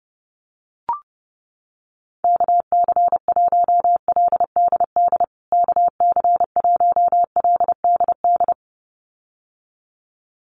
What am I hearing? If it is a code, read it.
Morse code, heard as a keyed tone, about 30 wpm, 700 Hz. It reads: KC1LDD KC1LDD
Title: USA Call Signs Repeated 2x 30wpm